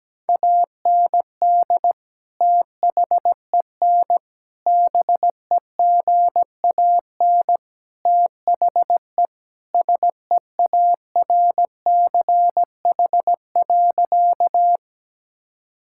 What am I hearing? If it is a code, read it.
Morse code, heard as a keyed tone, about 17 wpm, 700 Hz. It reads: AND THEN BEGAN THE SEARCH.
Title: Sentences from Top 1000 Words - Review Part 5 - 17wpm